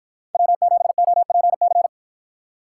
Morse code, heard as a keyed tone, about 53 wpm, 700 Hz. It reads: W7OPC